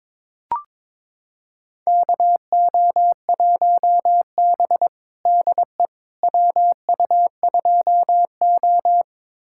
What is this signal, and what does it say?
Morse code, heard as a keyed tone, about 22 wpm, 700 Hz. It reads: KO1B DE WU2O
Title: USA Callsign DE Callsign - 4 Characters - 22wpm